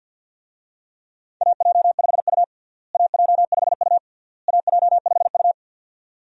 Morse code, heard as a keyed tone, about 47 wpm, 700 Hz. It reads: AJ5U AJ5U AJ5U
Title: USA Call Signs - 4 Characters - Repeated 3x - 50wpm